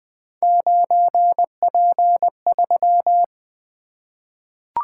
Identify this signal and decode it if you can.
Morse code, heard as a keyed tone, about 20 wpm, 700 Hz. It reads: 9P3
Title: Three Letter-Number Combinations - 20wpm